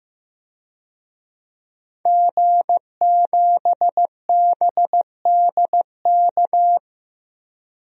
Morse code, heard as a keyed tone, about 15 wpm, 700 Hz. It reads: G7BDK
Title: DXCC-100 Call Signs - 5 Characters - 15wpm